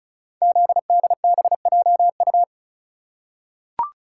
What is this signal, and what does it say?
Morse code, heard as a keyed tone, about 36 wpm, 700 Hz. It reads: ZDBJU
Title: Code Groups - 5 Letters - 35wpm